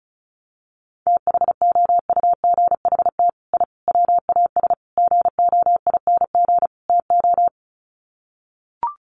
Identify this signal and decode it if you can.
Morse code, heard as a keyed tone, about 34 wpm, 700 Hz. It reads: THOUGHT I WAS GOING TO
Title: Sets of 5 Words Encore2 35wpm